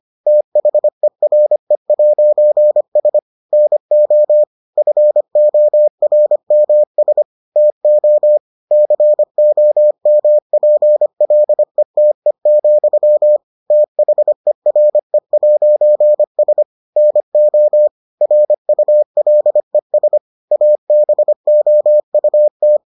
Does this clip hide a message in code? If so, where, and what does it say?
Yes: Morse code from 0.3 to 22.8 s, heard as a keyed tone, about 25 wpm, 600 Hz. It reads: THERE'S NO FORMS TO COMPLETE, THERE'S NO RULES ABOUT